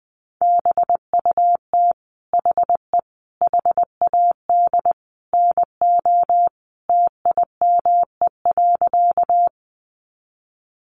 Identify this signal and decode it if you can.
Morse code, heard as a keyed tone, about 20 wpm, 700 Hz. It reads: BUT HE HAD NO TIME.
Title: Sentences from Top 100 Words 20wpm